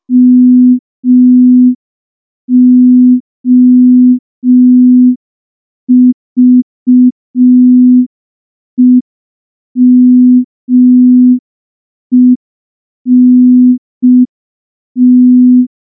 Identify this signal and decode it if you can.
Morse code, heard as a keyed tone, about 5 wpm, 250 Hz. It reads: MOVEMENT
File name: morse / morse-training-05WPM